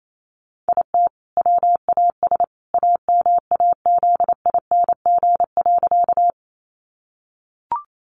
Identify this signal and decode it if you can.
Morse code, heard as a keyed tone, about 28 wpm, 700 Hz. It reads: IT WAS AMAZING.
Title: Sentences from Top 1500 Words 28wpm